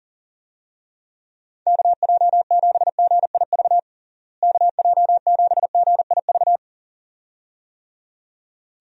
Morse code, heard as a keyed tone, about 40 wpm, 700 Hz. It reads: KJ7GIV KJ7GIV
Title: USA Call Signs - 6 Characters - Repeated 2x - 40wpm